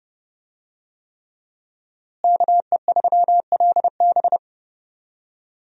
Morse code, heard as a keyed tone, about 28 wpm, 700 Hz. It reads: KE3LB
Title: USA Call Signs - 5 Characters - Repeated 3x - 30wpm